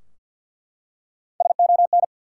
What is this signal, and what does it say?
Morse code, heard as a keyed tone, about 48 wpm, 700 Hz. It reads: SON
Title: Three Letter Words - 50wpm